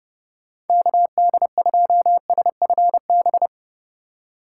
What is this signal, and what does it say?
Morse code, heard as a keyed tone, about 30 wpm, 700 Hz. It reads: KD2SFB
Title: USA Call Signs - 3x Character Spacing - 30wpm